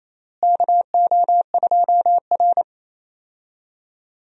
Morse code, heard as a keyed tone, about 28 wpm, 700 Hz. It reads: KO2R
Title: USA Call Signs - 4 Characters - Repeated 3x - 28wpm